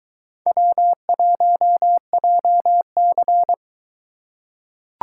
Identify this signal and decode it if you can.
Morse code, heard as a keyed tone, about 22 wpm, 700 Hz. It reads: W1JC